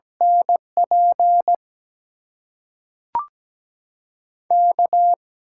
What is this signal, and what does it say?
Morse code, heard as a keyed tone, about 17 wpm, 700 Hz. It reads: NP K
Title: USA Call Signs - 6 Characters - 8x Character Spacing - 17wpm